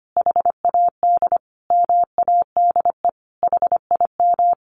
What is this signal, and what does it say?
Morse code, heard as a keyed tone, about 25 wpm, 700 Hz. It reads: HAD MADE HIM
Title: Sentences from Top 1400 Words 25wpm